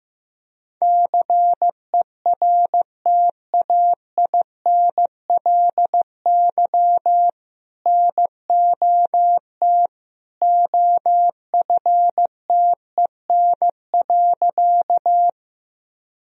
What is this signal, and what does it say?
Morse code, heard as a keyed tone, about 15 wpm, 700 Hz. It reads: CERTAINLY NOT OFTEN.
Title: Sentences from Top 1000 Words - Review Part 1 - 15wpm